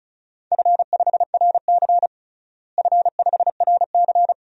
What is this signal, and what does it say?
Morse code, heard as a keyed tone, about 36 wpm, 700 Hz. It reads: F5RC F5RC
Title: DXCC-100 Call Signs - 4 Characters - Repeated 2x - 35wpm